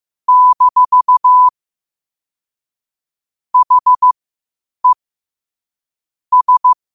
Morse code, heard as a keyed tone, about 15 wpm, spaced as for 8 wpm, 1000 Hz. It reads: - HE S
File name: qotdfiles / 8wpm